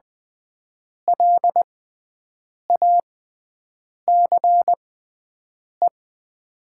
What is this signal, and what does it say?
Morse code, heard as a keyed tone, about 20 wpm, spaced as for 7 wpm, 700 Hz. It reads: LACE
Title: 6x Character Spacing - Top 300 Words - 20wpm